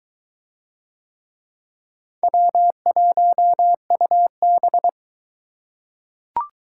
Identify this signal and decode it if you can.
Morse code, heard as a keyed tone, about 22 wpm, 700 Hz. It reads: W1UB